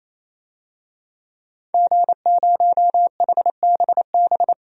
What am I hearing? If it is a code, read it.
Morse code, heard as a keyed tone, about 28 wpm, 700 Hz. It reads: G0HBB
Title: DXCC-100 Call Signs - 5 Characters - 28wpm